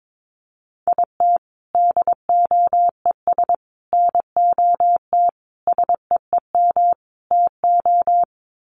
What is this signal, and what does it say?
Morse code, heard as a keyed tone, about 22 wpm, 700 Hz. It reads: IT DOES NOT SEEM TO
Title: Sets of 5 Words Encore2 22wpm